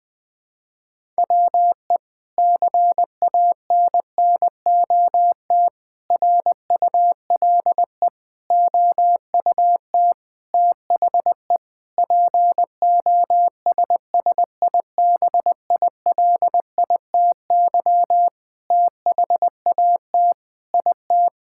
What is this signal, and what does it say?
Morse code, heard as a keyed tone, about 20 wpm, 700 Hz. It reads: WE CANNOT RULE OUT THE POSSIBILITY THAT IT
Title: Sentences with 8-Word Phrases - Spoken Beginning - 20wpm